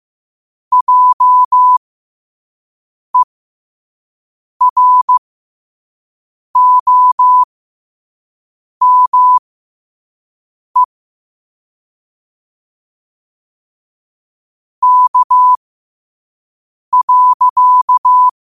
Morse code, heard as a keyed tone, about 15 wpm, spaced as for 5 wpm, 1000 Hz. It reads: JEROME K.